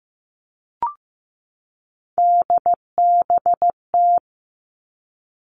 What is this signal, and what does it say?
Morse code, heard as a keyed tone, about 15 wpm, 700 Hz. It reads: DBT
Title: Three Challenging Letter-Number Combinations 15wpm